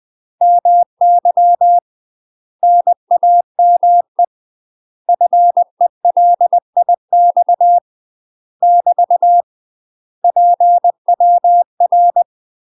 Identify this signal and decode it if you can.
Morse code, heard as a keyed tone, about 20 wpm, 700 Hz. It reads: MY NAME FELIX <BT> PWR
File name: UK MASTER QSOs TRAINING / 20WPM / UK MASTER QSOs 0038